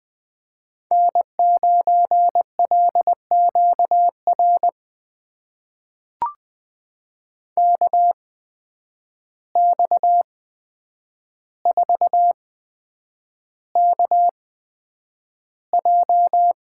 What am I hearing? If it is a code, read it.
Morse code, heard as a keyed tone, about 21 wpm, spaced as for 9 wpm, 700 Hz. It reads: N9LQR KX4KJ
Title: USA Call Signs - 5 Characters - 8x Character Spacing - 20wpm